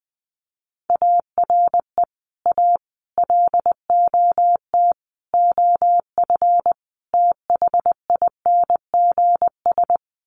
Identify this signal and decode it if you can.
Morse code, heard as a keyed tone, about 20 wpm, 700 Hz. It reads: ARE A LOT OF THINGS
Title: Sets of 5 Words Encore2 20wpm